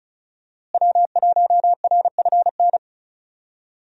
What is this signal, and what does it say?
Morse code, heard as a keyed tone, about 34 wpm, 700 Hz. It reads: W1RFN
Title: USA Call Signs - 5 Characters - Repeated 3x - 35wpm